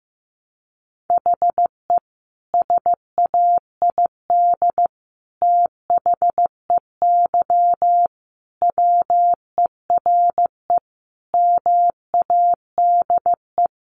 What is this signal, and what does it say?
Morse code, heard as a keyed tone, about 15 wpm, 700 Hz. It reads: HE SAID THEY WERE MADE
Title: Sentences from Top 700 Words 15wpm